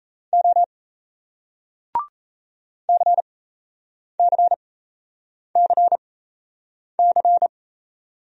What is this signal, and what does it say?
Morse code, heard as a keyed tone, about 38 wpm, spaced as for 8 wpm, 700 Hz. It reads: O CCCC